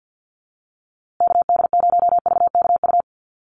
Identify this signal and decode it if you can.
Morse code, heard as a keyed tone, about 50 wpm, 700 Hz. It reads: KD0VKU
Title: USA Call Signs Repeated 2x 50wpm